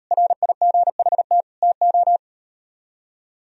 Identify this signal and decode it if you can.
Morse code, heard as a keyed tone, about 38 wpm, 700 Hz. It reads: RIGHT TO